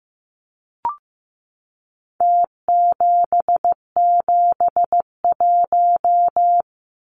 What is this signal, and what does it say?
Morse code, heard as a keyed tone, about 15 wpm, 700 Hz. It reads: T771
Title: Numbers - Contest Style - 4-Digit 15wpm